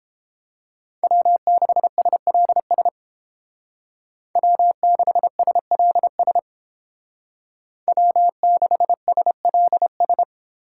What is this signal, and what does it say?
Morse code, heard as a keyed tone, about 29 wpm, 700 Hz. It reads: W6SLS W6SLS W6SLS